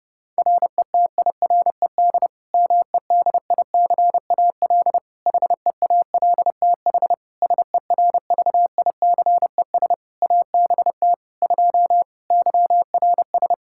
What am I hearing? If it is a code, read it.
Morse code, heard as a keyed tone, about 30 wpm, 700 Hz. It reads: RETIRED MEDICAL HEALTH SERVICES ABT 2 YRS
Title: Practice QSO - Exchange 3 - More Info - 30wpm